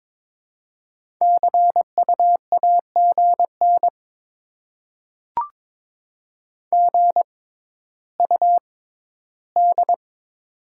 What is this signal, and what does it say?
Morse code, heard as a keyed tone, about 21 wpm, spaced as for 12 wpm, 700 Hz. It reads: CU AGN GUD
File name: Top 20 QSO Elements - 6x Character Spacing - 22wpm